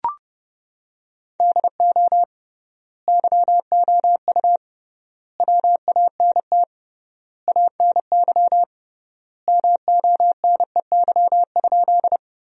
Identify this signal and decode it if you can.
Morse code, heard as a keyed tone, about 31 wpm, 700 Hz. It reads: DO YOU WANT ANY MONEY?
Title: 3x Word-Spacing - Sentences from Top 400 Words - 30wpm